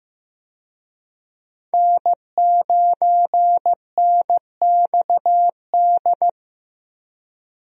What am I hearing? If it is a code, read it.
Morse code, heard as a keyed tone, about 16 wpm, 700 Hz. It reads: N9NXD
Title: USA Call Signs - 5 Characters - Repeated 2x - 15wpm